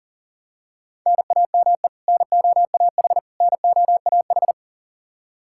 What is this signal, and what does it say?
Morse code, heard as a keyed tone, about 37 wpm, 700 Hz. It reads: NAME NOAH NOAH
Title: Practice QSO - Exchange 1 - Beginning - 40wpm